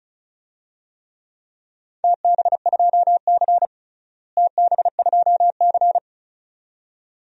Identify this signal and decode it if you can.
Morse code, heard as a keyed tone, about 36 wpm, 700 Hz. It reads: TB2C TB2C
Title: DXCC-100 Call Signs - 4 Characters - Repeated 2x - 35wpm